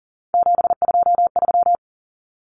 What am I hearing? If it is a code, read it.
Morse code, heard as a keyed tone, about 40 wpm, 700 Hz. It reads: 723
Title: Common Cut Numbers - 3-Digit 40wpm